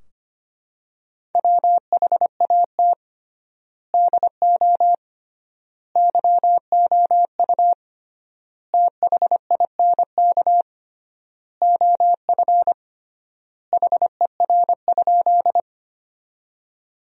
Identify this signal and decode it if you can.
Morse code, heard as a keyed tone, about 25 wpm, 700 Hz. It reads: WHAT DO YOU THINK OF HER?
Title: 3x Word-Spacing - Sentences from Top 200 Words - 25wpm